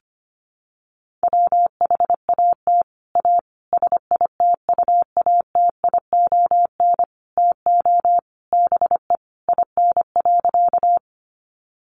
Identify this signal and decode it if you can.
Morse code, heard as a keyed tone, about 25 wpm, 700 Hz. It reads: WHAT A SITUATION TO BE IN.
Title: Sentences from Top 1200 Words 25wpm